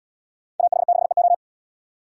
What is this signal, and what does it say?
Morse code, heard as a keyed tone, about 73 wpm, 700 Hz. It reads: AH6EO